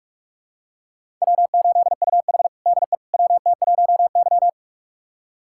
Morse code, heard as a keyed tone, about 46 wpm, 700 Hz. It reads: W8UH DE WT1Y
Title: USA Callsign DE Callsign - 4 Characters - 45wpm